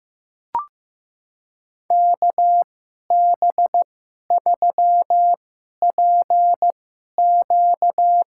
Morse code, heard as a keyed tone, about 15 wpm, spaced as for 11 wpm, 700 Hz. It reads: KB3PQ
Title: USA Call Signs - 5 Characters - 2x Character Spacing - 15wpm